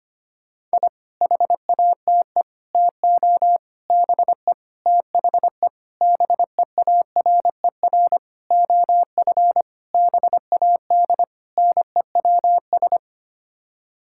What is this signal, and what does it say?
Morse code, heard as a keyed tone, about 25 wpm, 700 Hz. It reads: I HATE TO BE THE BEARER OF BAD NEWS
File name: Sets of 9 Words - 25wpm